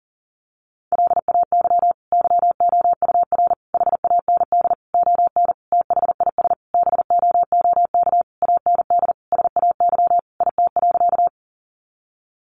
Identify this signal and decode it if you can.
Morse code, heard as a keyed tone, about 40 wpm, 700 Hz. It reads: LAY YOUR HAND ON THIS BOOK AND SAY IT.
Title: Sentences from Top 400 Words 40wpm